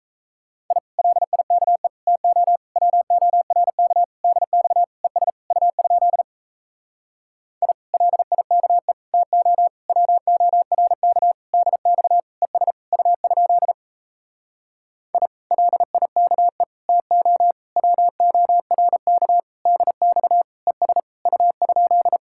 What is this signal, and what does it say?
Morse code, heard as a keyed tone, about 37 wpm, 700 Hz. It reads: I LIKE TO WORK DX ES U? I LIKE TO WORK DX ES U? I LIKE TO WORK DX ES U?